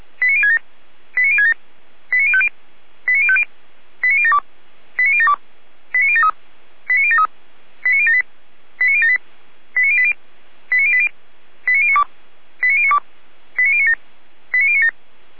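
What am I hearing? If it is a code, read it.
Five-tone selective call (ZVEI), heard as quick runs of five tones, tones of about 70 ms. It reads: 89067 89067 89050 89050 89072 89072 89073 89073 89078 89078 89080 89080 89092 89092 89097 89097
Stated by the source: ZVEI five-tone selective call 89760, received on SDR.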